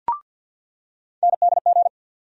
Morse code, heard as a keyed tone, about 53 wpm, 700 Hz. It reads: NDG
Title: USA Call Sign Postfixes - 3 Letters 50wpm